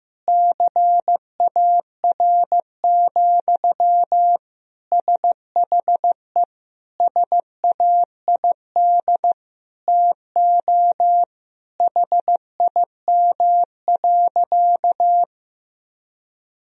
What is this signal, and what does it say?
Morse code, heard as a keyed tone, about 15 wpm, 700 Hz. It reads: CAR, SHE SAID TO HIM.
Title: Sentences from Top 1000 Words - Review Part 1 - 15wpm